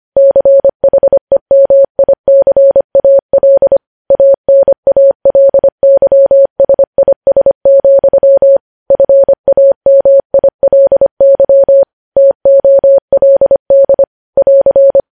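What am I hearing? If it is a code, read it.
Morse code, heard as a keyed tone, about 25 wpm, 550 Hz. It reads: CHEMICAL ANALYSIS, FAMILY TOLD <AR>
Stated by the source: Morse code news headlines at 25 words per minute.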